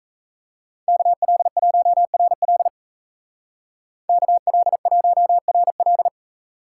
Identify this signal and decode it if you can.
Morse code, heard as a keyed tone, about 34 wpm, 700 Hz. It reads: KL1RL KL1RL
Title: Speed Racing - USA Call Signs 28wpm